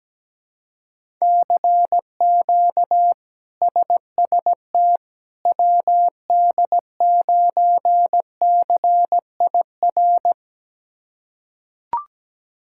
Morse code, heard as a keyed tone, about 17 wpm, 700 Hz. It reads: CQ SST WD9CIR
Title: K1USN SST QSO Practice - 17wpm